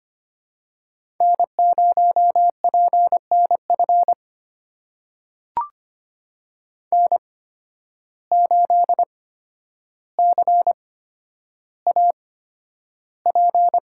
Morse code, heard as a keyed tone, about 25 wpm, spaced as for 11 wpm, 700 Hz. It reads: N0PNF N8CAP
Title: USA Call Signs - 5 Characters - 8x Character Spacing - 25wpm